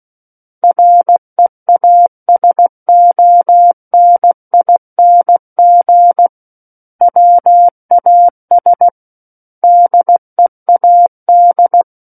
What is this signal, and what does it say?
Morse code, heard as a keyed tone, about 16 wpm, 700 Hz. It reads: REASONING WAS DEAD
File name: Book One / 07-Chapter 7 - 16 WPM